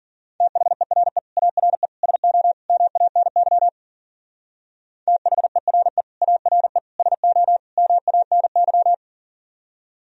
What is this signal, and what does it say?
Morse code, heard as a keyed tone, about 43 wpm, 700 Hz. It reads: THERE ARE SO MANY THERE ARE SO MANY
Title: Speed-Racing - Sets of 4 Words 40wpm